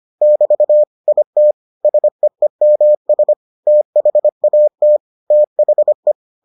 Morse code, heard as a keyed tone, about 25 wpm, 600 Hz. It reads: <BT> IT SEEMS THAT THE